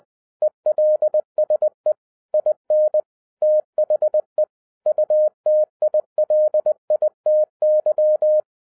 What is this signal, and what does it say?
Morse code, heard as a keyed tone, about 20 wpm, 610 Hz. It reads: ELSE IN THE UTILITY
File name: 2011-07-mp3 / n08 A LOT HAS CHANGED 20wpm